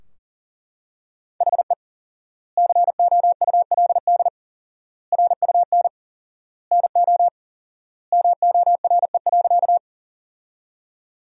Morse code, heard as a keyed tone, about 37 wpm, 700 Hz. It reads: HE COULD RUN NO MORE.
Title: 4x Word-Spacing - Sentences from Top 300 Words - 40wpm